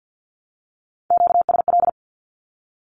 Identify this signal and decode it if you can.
Morse code, heard as a keyed tone, about 50 wpm, 700 Hz. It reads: QSL
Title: Q-Codes Repeated 3x 50wpm